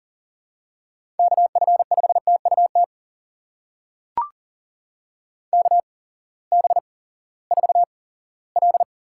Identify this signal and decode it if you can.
Morse code, heard as a keyed tone, about 42 wpm, spaced as for 20 wpm, 700 Hz. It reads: KF5TUT KB4L
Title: USA Call Signs - 6 Characters - 8x Character Spacing - 40wpm